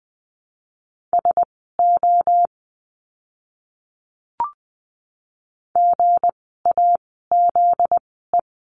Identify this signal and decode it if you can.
Morse code, heard as a keyed tone, about 20 wpm, spaced as for 14 wpm, 700 Hz. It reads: SO GAZE
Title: 244 Lesson 39f - Letter z - Words 20wpm